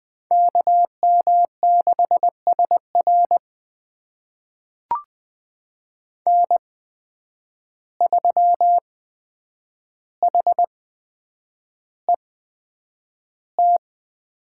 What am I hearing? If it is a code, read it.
Morse code, heard as a keyed tone, about 20 wpm, spaced as for 9 wpm, 700 Hz. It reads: KM6SR N3HET
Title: USA Call Signs - 5 Characters - 8x Character Spacing - 20wpm